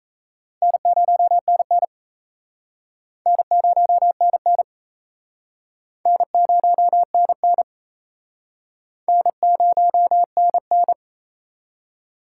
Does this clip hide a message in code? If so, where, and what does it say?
Yes: Morse code from 0.6 to 10.9 s, heard as a keyed tone, about 36 wpm, 700 Hz. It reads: N0NN N0NN N0NN N0NN